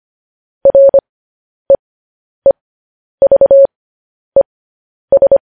Morse code, heard as a keyed tone, about 25 wpm, spaced as for 10 wpm, 550 Hz. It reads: REEVES
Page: news-headlines-in-morse-code-at-10-wpm